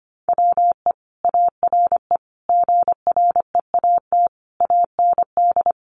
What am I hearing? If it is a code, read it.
Morse code, heard as a keyed tone, about 25 wpm, 700 Hz. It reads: WE ARE GREAT AND
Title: Sentences from Top 1300 Words 25wpm